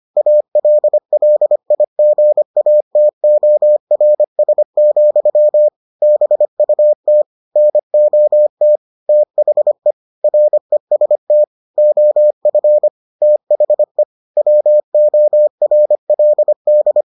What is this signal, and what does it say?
Morse code, heard as a keyed tone, about 25 wpm, 600 Hz. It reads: ALLIGATORS, BUT NOT THE REST OF THE WORLD